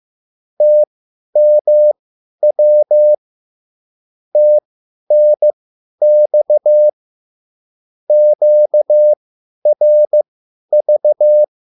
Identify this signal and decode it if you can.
Morse code, heard as a keyed tone, about 15 wpm, spaced as for 10 wpm, 600 Hz. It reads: TMW TNX QRV
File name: QSL / CW lesson